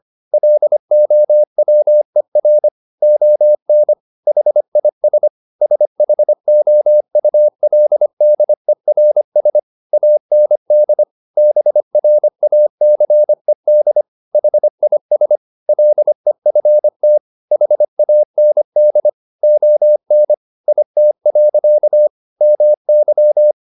Morse code, heard as a keyed tone, about 25 wpm, 600 Hz. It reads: LOWER ON HIS SHOULDERS AND BRACED HIS LEFT HAND ON IT. MY